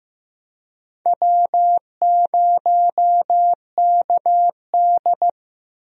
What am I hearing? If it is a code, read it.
Morse code, heard as a keyed tone, about 15 wpm, 700 Hz. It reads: W0KD
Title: USA Call Signs - 4 Characters - 15wpm